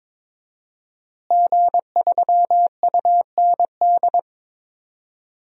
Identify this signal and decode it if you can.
Morse code, heard as a keyed tone, about 22 wpm, 700 Hz. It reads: G3UND
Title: DXCC-100 Call Signs - 5 Characters - Repeated 3x - 22wpm